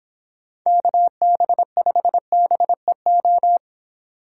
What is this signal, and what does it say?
Morse code, heard as a keyed tone, about 26 wpm, 700 Hz. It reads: KB5BEO